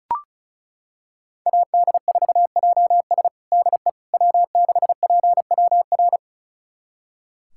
Morse code, heard as a keyed tone, about 34 wpm, 700 Hz. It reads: AD4JS DE W6PWR
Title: USA Callsign DE Callsign - 5 Characters - 35wpm